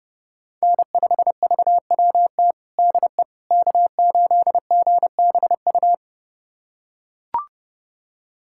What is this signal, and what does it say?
Morse code, heard as a keyed tone, about 30 wpm, 700 Hz. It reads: N5VWT DE K8GBU
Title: USA Callsign DE Callsign - 5 Characters - 30wpm